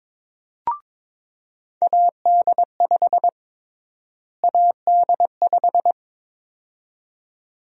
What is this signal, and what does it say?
Morse code, heard as a keyed tone, about 22 wpm, 700 Hz. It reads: AD5 AD5
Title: USA Call Sign Prefixes - Repeated 2x - 22wpm